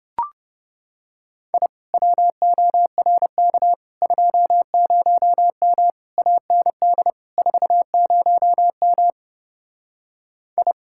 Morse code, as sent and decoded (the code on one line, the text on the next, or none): .. / .-- --- .-. -.- / ..--- ----- -- / .- -. -.. / ....- ----- -- / ..
I WORK 20M AND 40M I